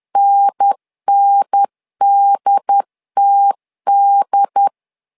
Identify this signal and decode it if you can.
Morse code, heard as a keyed tone, about 10 wpm, 800 Hz. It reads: NNDTD
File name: LEZIONI / 2 UHDN / 2 UHDN 10wpm 1904